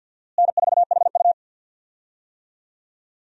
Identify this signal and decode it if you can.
Morse code, heard as a keyed tone, about 52 wpm, 700 Hz. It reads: N4HU
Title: USA Call Signs - 4 Characters - Repeated 3x - 50wpm